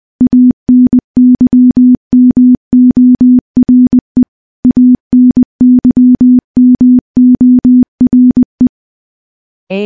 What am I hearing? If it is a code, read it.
Morse code, heard as a keyed tone, about 21 wpm, 250 Hz. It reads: ANYMORE ANYMORE
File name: morse-training-20WPM